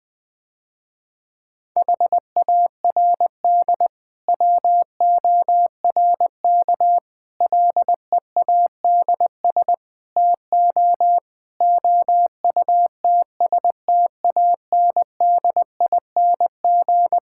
Morse code, heard as a keyed tone, about 20 wpm, 700 Hz. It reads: HARD WORK LEADS TO OUTSTANDING
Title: Difficult Sentences with Hints - 20wpm